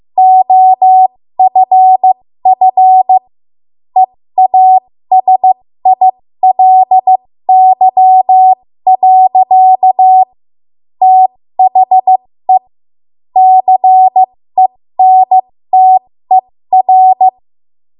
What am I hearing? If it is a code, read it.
Morse code, heard as a keyed tone, about 15 wpm, spaced as for 13 wpm, 750 Hz. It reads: OFF EASILY. THE CENTER
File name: Morse / 13WPM